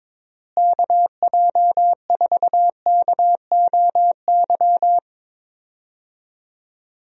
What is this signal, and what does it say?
Morse code, heard as a keyed tone, about 22 wpm, 700 Hz. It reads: KJ4KOY
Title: USA Call Signs - 6 Characters - Repeated 3x - 22wpm